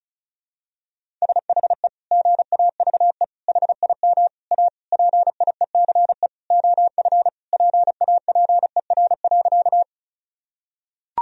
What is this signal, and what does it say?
Morse code, heard as a keyed tone, about 34 wpm, 700 Hz. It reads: SHE GAVE HIM A PIECE OF PAPER.
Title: Sentences from Top 1000 Words - Review Part 2 - 35wpm